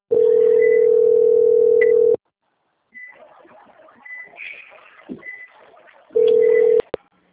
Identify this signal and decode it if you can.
Ringback tone (North America), 440 + 480 Hz.
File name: out- - - - - .21929